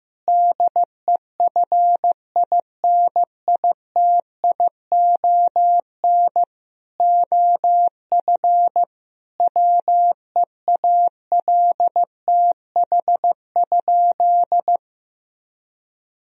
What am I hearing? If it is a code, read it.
Morse code, heard as a keyed tone, about 15 wpm, 700 Hz. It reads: DEFINITION OF WEALTH?
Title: Sentences from Top 1000 Words - Review Part 4 - 15wpm